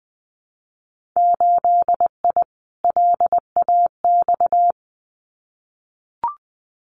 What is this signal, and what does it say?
Morse code, heard as a keyed tone, about 20 wpm, 700 Hz. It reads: 8I LAX
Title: Winter Field Day Exchanges Encore 20wpm